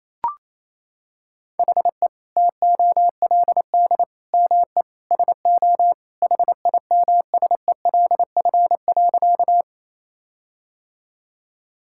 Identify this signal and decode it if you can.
Morse code, heard as a keyed tone, about 28 wpm, 700 Hz. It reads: HE TOLD ME SO HIMSELF.
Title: Sentences from Top 1600 Words 28wpm